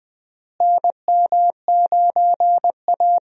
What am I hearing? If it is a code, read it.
Morse code, heard as a keyed tone, about 20 wpm, 700 Hz. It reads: NM9A
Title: USA Call Signs - 4 Characters - Repeated 3x - 20wpm